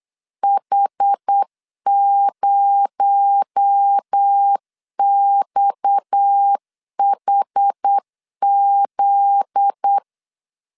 Morse code, heard as a keyed tone, about 8 wpm, 800 Hz. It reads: H0XHZ
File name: LEZIONI / 7 GF9Z / 7 GF9Z 8wpm 3213